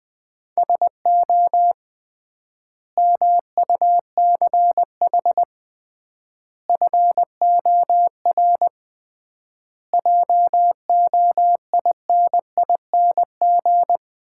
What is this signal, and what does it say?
Morse code, heard as a keyed tone, about 21 wpm, 700 Hz. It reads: SO MUCH FOR JOINING
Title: 3x Word-Spacing - Sets of 5 Words - 20wpm